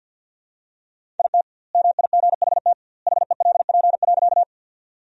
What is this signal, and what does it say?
Morse code, heard as a keyed tone, about 50 wpm, 700 Hz. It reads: IT MIGHT HELP.
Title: 2x Word-Spacing - Sentences from Top 300 Words - 50wpm